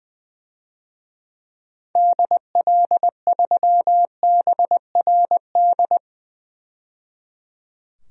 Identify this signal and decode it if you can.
Morse code, heard as a keyed tone, about 20 wpm, 700 Hz. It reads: DL3BRD
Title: DXCC-100 Call Signs - 6 Characters - 20wpm